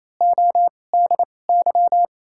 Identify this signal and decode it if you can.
Morse code, heard as a keyed tone, about 28 wpm, spaced as for 20 wpm, 700 Hz. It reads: ODY
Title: 2x Character Spacing - Top 500 Words - 28wpm